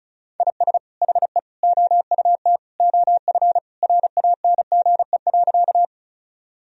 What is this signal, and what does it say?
Morse code, heard as a keyed tone, about 36 wpm, 700 Hz. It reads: IS HE OUT OF RANGE.
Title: Sentences from Top 1000 Words - Review Part 4 - 35wpm